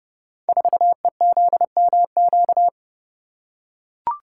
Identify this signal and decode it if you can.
Morse code, heard as a keyed tone, about 28 wpm, 700 Hz. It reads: 4EZMQ
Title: Code Groups - 5 Letter:Numbers - 30wpm